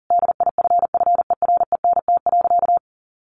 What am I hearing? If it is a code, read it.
Morse code, heard as a keyed tone, about 40 wpm, 700 Hz. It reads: DIFFERENT.
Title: Sentences from Top 1300 Words 40wpm